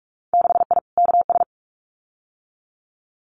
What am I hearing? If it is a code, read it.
Morse code, heard as a keyed tone, about 44 wpm, 700 Hz. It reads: 6I KS
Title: Winter Field Day Exchanges 45wpm